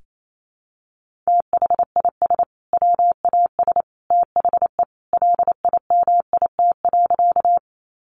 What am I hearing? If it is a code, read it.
Morse code, heard as a keyed tone, about 28 wpm, 700 Hz. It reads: THIS WAS THE LIMIT.
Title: Sentences from Top 1500 Words 28wpm